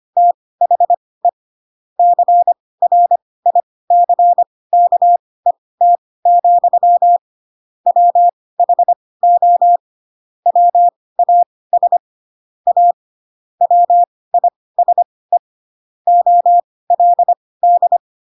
Morse code, heard as a keyed tone, about 25 wpm, spaced as for 18 wpm, 700 Hz. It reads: THE CRICKET, WHO WAS A WISE OLD
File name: Pinocchio / Pinocchio ch4 12 25 ik1hge